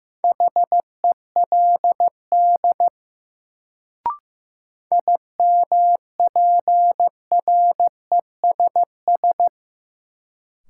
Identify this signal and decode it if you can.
Morse code, heard as a keyed tone, about 15 wpm, 700 Hz. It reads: HELD IMPRESS
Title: Sentences from Top 1600 Words - New Word Drill - 15wpm